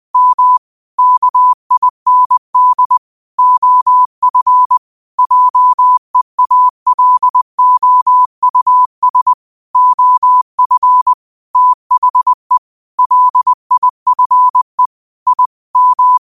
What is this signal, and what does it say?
Morse code, heard as a keyed tone, about 20 wpm, 1000 Hz. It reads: M KIND OF JEALOUS OF THE LIFE I M
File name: qotdfiles / 20wpm